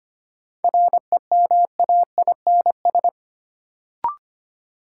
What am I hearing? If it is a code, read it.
Morse code, heard as a keyed tone, about 25 wpm, 700 Hz. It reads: REMAINS
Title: Sets of 5 Words Encore - New Word Drill - 25wpm